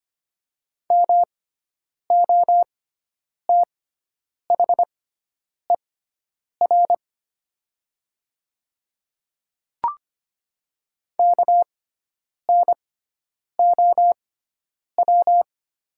Morse code, heard as a keyed tone, about 25 wpm, spaced as for 9 wpm, 700 Hz. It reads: MOTHER KNOW
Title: 6x Character Spacing - Top 200 Words - 25wpm